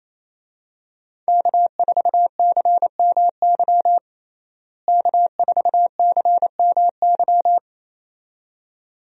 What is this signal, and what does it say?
Morse code, heard as a keyed tone, about 28 wpm, 700 Hz. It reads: K4CMY K4CMY
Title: USA Call Signs - 5 Characters - Repeated 2x - 28wpm